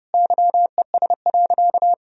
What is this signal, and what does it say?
Morse code, heard as a keyed tone, about 30 wpm, 700 Hz. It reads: YES.
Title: Sentences from Top 1000 Words - Review Part 1 - 30wpm